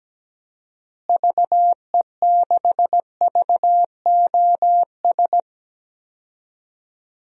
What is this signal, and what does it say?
Morse code, heard as a keyed tone, about 17 wpm, 700 Hz. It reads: VE6VOS
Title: DXCC-100 Call Signs - 6 Characters - Repeated 2x - 17wpm